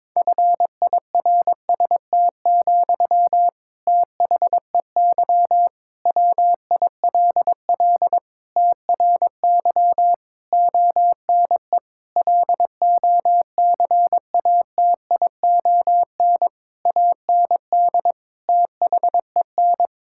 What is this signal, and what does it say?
Morse code, heard as a keyed tone, about 22 wpm, 700 Hz. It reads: FIRST, THEY WILL TRY ONE LOCATION AND THEN
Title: Sentences from Top 1000 Words - Review Part 6 - 22wpm